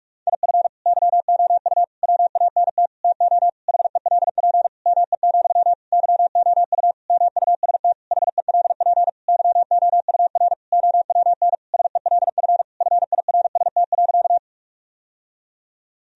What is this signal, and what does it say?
Morse code, heard as a keyed tone, about 41 wpm, 700 Hz. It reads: IF YOU WANT TO HELP ME, YOU MUST HELP YOUR OWN SELF FIRST.
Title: Sentences from Top 1000 Words - Review Part 1 - 45wpm